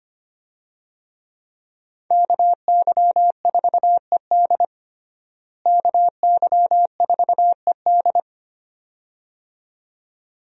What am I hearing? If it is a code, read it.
Morse code, heard as a keyed tone, about 25 wpm, 700 Hz. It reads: KY4ED KY4ED
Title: USA Call Signs - 5 Characters - Repeated 2x - 25wpm